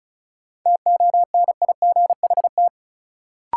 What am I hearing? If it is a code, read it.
Morse code, heard as a keyed tone, about 36 wpm, 700 Hz. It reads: TONIGHT
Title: Sentences from Top 600 Words - New Word Drill - 35wpm